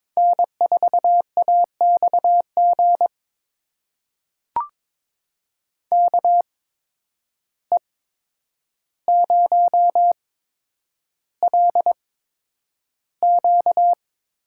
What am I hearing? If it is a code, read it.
Morse code, heard as a keyed tone, about 21 wpm, spaced as for 9 wpm, 700 Hz. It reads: N4AXG KE0LQ
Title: USA Call Signs - 5 Characters - 8x Character Spacing - 22wpm